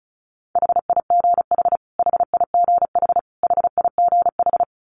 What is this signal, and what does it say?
Morse code, heard as a keyed tone, about 34 wpm, 700 Hz. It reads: HIGH HIGH HIGH
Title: Top 500 Words Repeated 35wpm